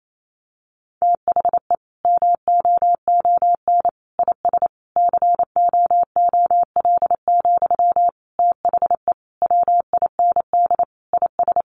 Morse code, heard as a keyed tone, about 28 wpm, 700 Hz. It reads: THE MOON IS COOL, THE WIND IS
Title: Sentences from Top 1200 Words 28wpm